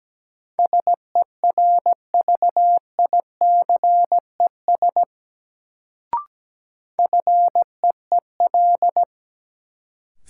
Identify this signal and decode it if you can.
Morse code, heard as a keyed tone, about 17 wpm, 700 Hz. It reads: SERVICES FEEL
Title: Sets of 5 Words - New Word Drill - 17wpm